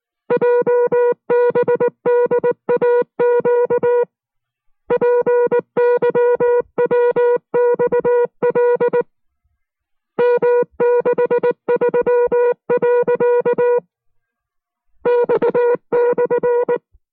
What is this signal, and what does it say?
Morse code, heard as a keyed tone, about 19 wpm, spaced as for 16 wpm, 480 Hz. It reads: JBDAQ PYWXL M63.X /
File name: FreeFiles / CW-practice / 15WPM-side1